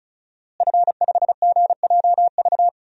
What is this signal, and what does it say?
Morse code, heard as a keyed tone, about 34 wpm, 700 Hz. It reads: F5GJV
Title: DXCC-100 Call Signs - 5 Characters - Repeated 3x - 35wpm